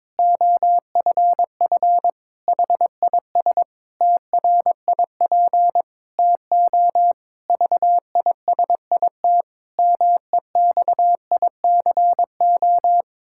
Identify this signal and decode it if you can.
Morse code, heard as a keyed tone, about 22 wpm, 700 Hz. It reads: OFF HIS TRIP TO VISIT MEXICO
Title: Pangram Challenge v2 - 22wpm